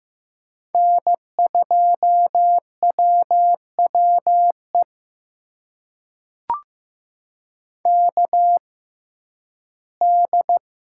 Morse code, heard as a keyed tone, about 15 wpm, spaced as for 11 wpm, 700 Hz. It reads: N2WWE KD
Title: USA Call Signs - 5 Characters - 6x Character Spacing - 15wpm